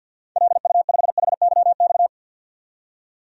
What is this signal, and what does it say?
Morse code, heard as a keyed tone, about 51 wpm, 700 Hz. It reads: LU5HYX
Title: DXCC-100 Call Signs - 6 Characters - 50wpm